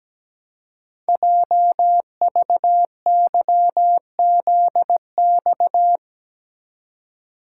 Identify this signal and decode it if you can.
Morse code, heard as a keyed tone, about 17 wpm, 700 Hz. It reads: JVYZX
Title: Code Groups - 5 Letters - 17wpm